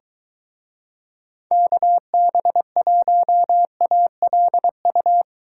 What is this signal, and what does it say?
Morse code, heard as a keyed tone, about 23 wpm, 700 Hz. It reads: KB1ALU